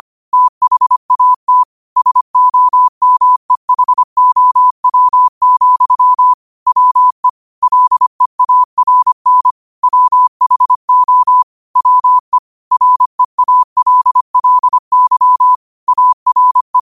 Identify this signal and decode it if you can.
Morse code, heard as a keyed tone, about 25 wpm, 1000 Hz. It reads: THAT SOMEHOW, WE LEARN WHO WE REALLY ARE
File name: qotdfiles / 25wpm